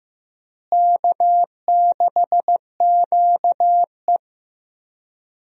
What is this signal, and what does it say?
Morse code, heard as a keyed tone, about 15 wpm, 700 Hz. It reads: K6QE
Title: USA Call Signs - 4 Characters - Repeated 3x - 15wpm